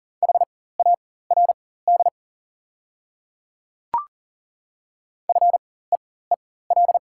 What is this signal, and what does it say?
Morse code, heard as a keyed tone, about 40 wpm, spaced as for 19 wpm, 690 Hz. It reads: HARD FEEL
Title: 4x Character Spacing - Top 500 Words - 40wpm